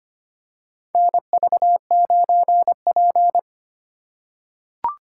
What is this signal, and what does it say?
Morse code, heard as a keyed tone, about 25 wpm, 700 Hz. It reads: NV9P
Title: USA Call Signs - 4 Characters - 2x Character Spacing - 25wpm